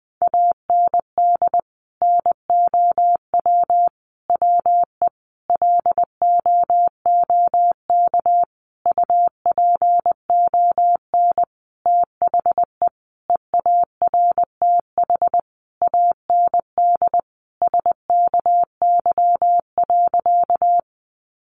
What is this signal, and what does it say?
Morse code, heard as a keyed tone, about 20 wpm, 700 Hz. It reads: AND NOW WE LOOK UPON THE EARTH AND SKY.
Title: Sentences from Top 1200 Words 20wpm